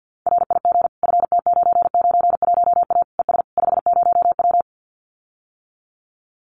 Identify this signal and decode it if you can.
Morse code, heard as a keyed tone, about 50 wpm, 700 Hz. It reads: RIG FT991A ES 50W
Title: Common QSO Phrases Encore 50wpm